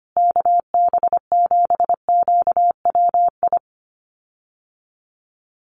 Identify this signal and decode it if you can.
Morse code, heard as a keyed tone, about 25 wpm, 700 Hz. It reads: KB7QWI
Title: USA Call Signs Repeated 3x 25wpm